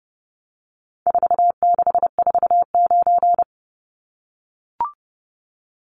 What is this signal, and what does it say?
Morse code, heard as a keyed tone, about 30 wpm, 700 Hz. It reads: 4649
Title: Numbers - 4-Digit 30wpm